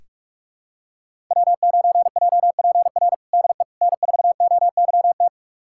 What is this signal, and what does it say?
Morse code, heard as a keyed tone, about 46 wpm, 700 Hz. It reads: W9JPR DE N4OYT
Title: USA Callsign DE Callsign - 5 Characters - 45wpm